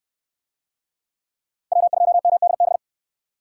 Morse code, heard as a keyed tone, about 59 wpm, 700 Hz. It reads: W2MDB